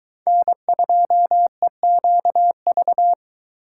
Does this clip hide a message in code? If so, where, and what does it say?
Yes: Morse code from 0.3 to 3.1 s, heard as a keyed tone, about 23 wpm, 700 Hz. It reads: N2EQV